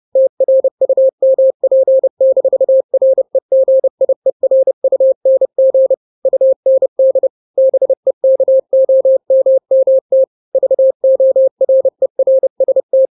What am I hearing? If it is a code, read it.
Morse code, heard as a keyed tone, about 30 wpm, 530 Hz. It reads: TRUMP-REGIERUNG UND BEKOMMT VORERST
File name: 29wpm